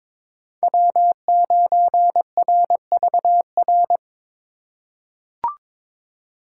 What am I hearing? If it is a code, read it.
Morse code, heard as a keyed tone, about 22 wpm, 700 Hz. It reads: W9RVR
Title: USA Call Signs - 5 Characters - 3x Character Spacing - 22wpm